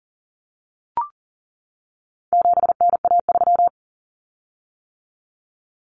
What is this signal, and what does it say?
Morse code, heard as a keyed tone, about 40 wpm, 700 Hz. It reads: 7NA3
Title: Common Cut Numbers - 4-Digit 40wpm